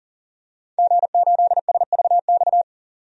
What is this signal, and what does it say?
Morse code, heard as a keyed tone, about 40 wpm, 700 Hz. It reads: G8SVX
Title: DXCC-100 Call Signs - 5 Characters - 40wpm